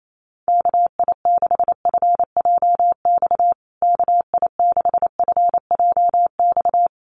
Morse code, heard as a keyed tone, about 28 wpm, 700 Hz. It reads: KI6FJX KI6FJX
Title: USA Call Signs Repeated 3x 28wpm